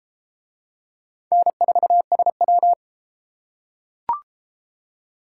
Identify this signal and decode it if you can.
Morse code, heard as a keyed tone, about 31 wpm, 700 Hz. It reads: N4SW